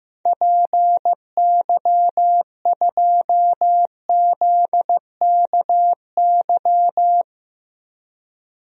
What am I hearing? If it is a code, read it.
Morse code, heard as a keyed tone, about 16 wpm, 700 Hz. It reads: PY2ZKY
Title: DXCC-100 Call Signs - 6 Characters - Repeated 3x - 15wpm